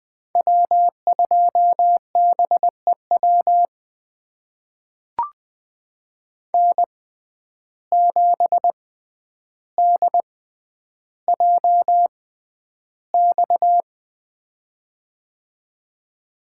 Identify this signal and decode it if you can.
Morse code, heard as a keyed tone, about 20 wpm, spaced as for 10 wpm, 700 Hz. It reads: W2BEW N7DJX
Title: USA Call Signs - 5 Characters - 6x Character Spacing - 20wpm